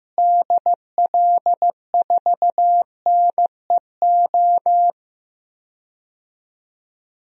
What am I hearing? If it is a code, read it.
Morse code, heard as a keyed tone, about 15 wpm, 700 Hz. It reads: DL4NEO
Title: DXCC-100 Call Signs - 6 Characters - Repeated 2x - 15wpm